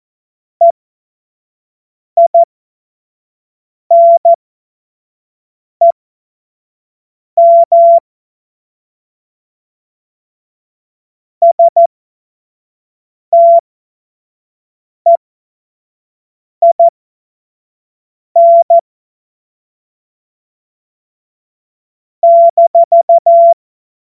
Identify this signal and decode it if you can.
Morse code, heard as a keyed tone, about 14 wpm, spaced as for 5 wpm, 680 Hz. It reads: EINEM STEIN -
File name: Morsekurs / Witze / Witz 01. August 2023 25 F70